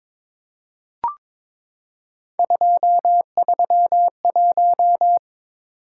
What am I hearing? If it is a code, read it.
Morse code, heard as a keyed tone, about 21 wpm, 700 Hz. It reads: 231
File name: practice / 22wpm / Three Letter-Number Combinations - 22wpm